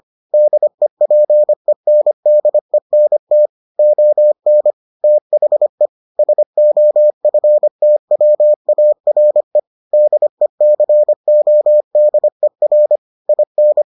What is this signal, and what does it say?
Morse code, heard as a keyed tone, about 25 wpm, 600 Hz. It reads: DEPENDENT ON THE SOFTWARE DECODER IN